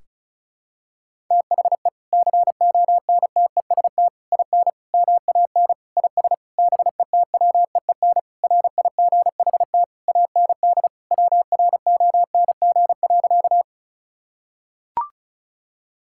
Morse code, heard as a keyed tone, about 36 wpm, 700 Hz. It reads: THE CONTEST IN MAN IS BETWEEN RIGHT AND WRONG.
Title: Sentences from Top 1600 Words 35wpm